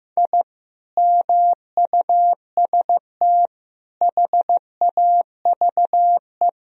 Morse code, heard as a keyed tone, about 15 wpm, 700 Hz. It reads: I MUST HAVE
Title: Sentences from Top 1000 Words - Review Part 3 - 15wpm